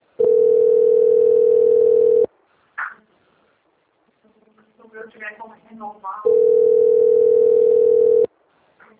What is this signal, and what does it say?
Ringback tone (North America), 440 + 480 Hz, with speech.
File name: rg-1600-unknown- - - .14342